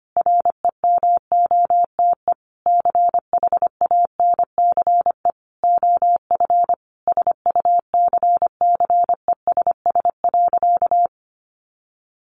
Morse code, heard as a keyed tone, about 25 wpm, 700 Hz. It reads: REMOTE CHANCE OF SUCCESS.
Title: Sentences from Top 1300 Words 25wpm